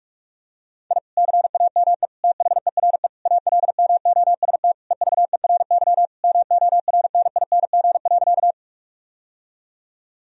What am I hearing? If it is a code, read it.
Morse code, heard as a keyed tone, about 46 wpm, 700 Hz. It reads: I CAME THERE ALMOST EVERY MORNING.
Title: Sentences from Top 1000 Words - Review Part 4 - 45wpm